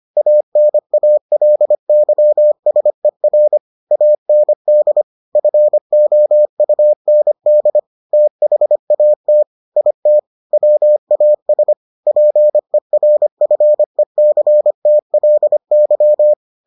Morse code, heard as a keyed tone, about 25 wpm, 600 Hz. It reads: ANALYSER AND FOUND THAT IT WAS PERFECTLY